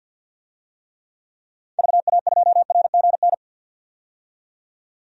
Morse code, heard as a keyed tone, about 52 wpm, 700 Hz. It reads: VA2RGN
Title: DXCC-100 Call Signs - 6 Characters - 50wpm